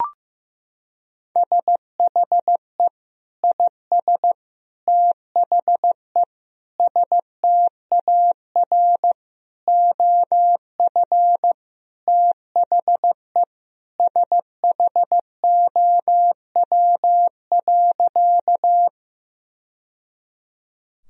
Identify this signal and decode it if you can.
Morse code, heard as a keyed tone, about 15 wpm, 700 Hz. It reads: SHE IS THE STAR OF THE SHOW.
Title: Sentences from Top 1000 Words - Review Part 2 - 15wpm